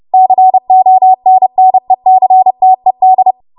Morse code, heard as a keyed tone, about 30 wpm, 750 Hz. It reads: CONNECTED
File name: Archive / 30 WPM / 30WPM